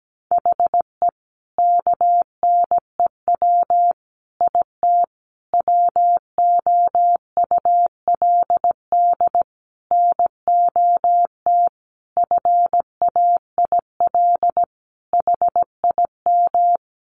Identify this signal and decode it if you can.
Morse code, heard as a keyed tone, about 17 wpm, 700 Hz. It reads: HE KNEW IT WOULD NOT FAIL HIM
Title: Sentences from Top 1100 Words 17wpm